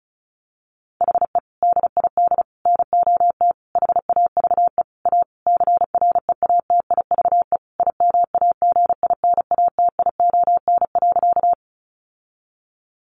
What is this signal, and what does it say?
Morse code, heard as a keyed tone, about 34 wpm, 700 Hz. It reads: HE DID NOT HAVE A CREATIVE IMAGINATION.
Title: Sentences from Top 1300 Words 35wpm